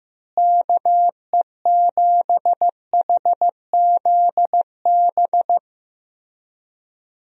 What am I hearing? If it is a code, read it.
Morse code, heard as a keyed tone, about 15 wpm, 700 Hz. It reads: KE7HZB
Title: USA Call Signs - 6 Characters - Repeated 2x - 15wpm